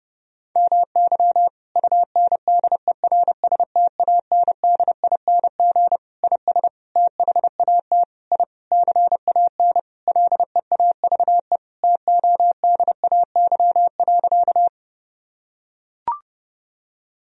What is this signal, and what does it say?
Morse code, heard as a keyed tone, about 32 wpm, 700 Hz. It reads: MY UNDERSTANDING IS THAT I CAN LEAVE TODAY.
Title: Sentences from Top 1000 Words - Review Part 3 - 30wpm